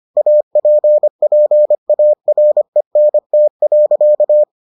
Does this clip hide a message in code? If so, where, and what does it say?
Yes: Morse code from 0.2 to 4.4 s, heard as a keyed tone, about 25 wpm, 600 Hz. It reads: APPARENT.